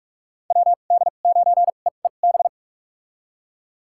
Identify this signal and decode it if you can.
Morse code, heard as a keyed tone, about 42 wpm, spaced as for 31 wpm, 700 Hz. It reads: WD9EEB
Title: USA Call Signs - 6 Characters - 2x Character Spacing - 45wpm